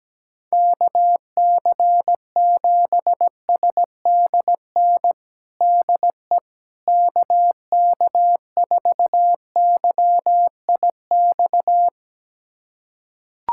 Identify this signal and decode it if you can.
Morse code, heard as a keyed tone, about 17 wpm, 700 Hz. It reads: KC7SDN DE KK4YIX
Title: USA Callsign DE Callsign - 6 Characters - 17wpm